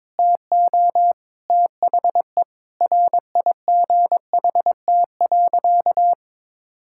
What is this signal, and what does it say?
Morse code, heard as a keyed tone, about 22 wpm, 700 Hz. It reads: TO THE RIGHT.
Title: Sentences from Top 1000 Words - Review Part 5 - 22wpm